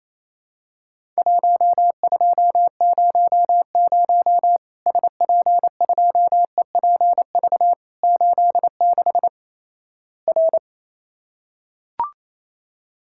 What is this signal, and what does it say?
Morse code, heard as a keyed tone, about 28 wpm, 700 Hz. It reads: 1200 SP2EPV 86 R
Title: Worked All Europe DX Contest - QTC Exchange - 28wpm